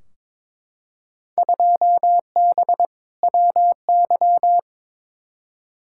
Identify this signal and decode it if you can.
Morse code, heard as a keyed tone, about 22 wpm, 700 Hz. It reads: 2B WY
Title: Field Day Exchanges Encore - 22wpm